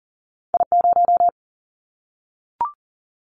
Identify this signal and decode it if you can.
Morse code, heard as a keyed tone, about 40 wpm, 700 Hz. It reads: I0
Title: Callsign Prefixes for Top 25 DXCC - 40wpm